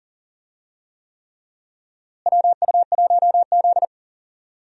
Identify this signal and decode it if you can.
Morse code, heard as a keyed tone, about 40 wpm, 700 Hz. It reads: WU1Z
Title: USA Call Signs - 4 Characters - Repeated 3x - 40wpm